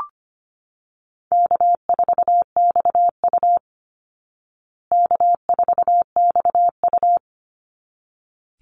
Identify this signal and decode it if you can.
Morse code, heard as a keyed tone, about 25 wpm, 700 Hz. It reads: K4XU K4XU
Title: Easy Call Signs Repeated 2x - 25wpm